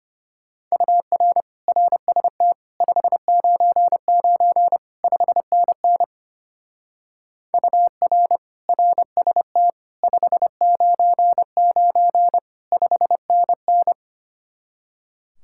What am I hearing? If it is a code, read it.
Morse code, heard as a keyed tone, about 27 wpm, 700 Hz. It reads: UR RST 599 5NN UR RST 599 5NN
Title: Speed-Racing Common QSO Phrases - 25wpm